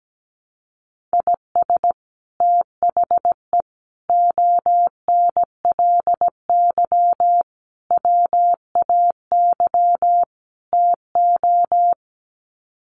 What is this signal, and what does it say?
Morse code, heard as a keyed tone, about 17 wpm, 700 Hz. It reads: IS THE ONLY WAY TO
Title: Sets of 5 Words Encore 17wpm